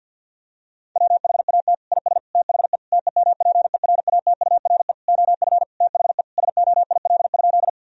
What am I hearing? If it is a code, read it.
Morse code, heard as a keyed tone, about 49 wpm, 700 Hz. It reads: WHAT IS THE TEMPERATURE OF THE SOIL?
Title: Sentences from Top 1000 Words - Review Part 4 - 50wpm